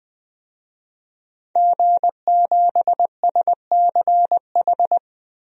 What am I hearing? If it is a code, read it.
Morse code, heard as a keyed tone, about 21 wpm, 700 Hz. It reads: G7SCH
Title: DXCC-100 Call Signs - 5 Characters - 20wpm